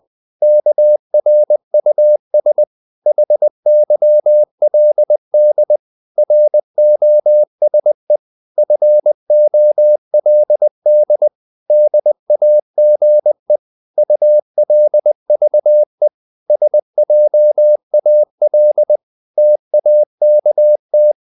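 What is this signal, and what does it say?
Morse code, heard as a keyed tone, about 20 wpm, 600 Hz. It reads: KRUS HYLD ROSE FOLD DAGE ULVE SJAL TAKT
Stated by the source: Morse code practice of short Danish words at 20 wpm, from CW med Gnister.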